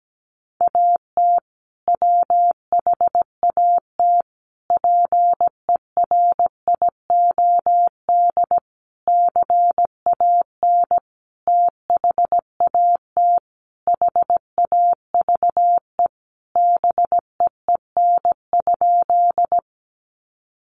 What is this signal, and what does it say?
Morse code, heard as a keyed tone, about 17 wpm, 700 Hz. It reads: AT WHAT PERIOD CAN THAT HAVE BEEN?
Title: Sentences from Top 800 Words 17wpm